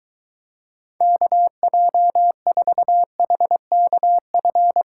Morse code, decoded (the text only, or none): KJ4HKF